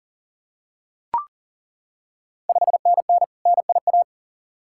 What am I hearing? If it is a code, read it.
Morse code, heard as a keyed tone, about 39 wpm, 700 Hz. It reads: 5NN NIA
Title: Ontario QSO Party Exchange 40wpm